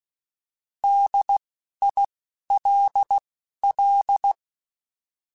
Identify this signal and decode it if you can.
Morse code, heard as a keyed tone, about 16 wpm, 800 Hz. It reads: D I L L
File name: Alphabetical / Alphabetical mp3s / Ds @10